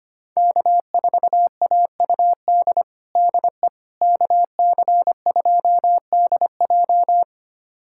Morse code, heard as a keyed tone, about 25 wpm, 700 Hz. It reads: K4AUD DE KC2DJ
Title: USA Callsign DE Callsign - 5 Characters - 25wpm